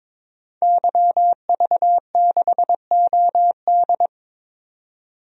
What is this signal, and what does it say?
Morse code, heard as a keyed tone, about 22 wpm, 700 Hz. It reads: YV6OD
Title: DXCC-100 Call Signs - 5 Characters - 22wpm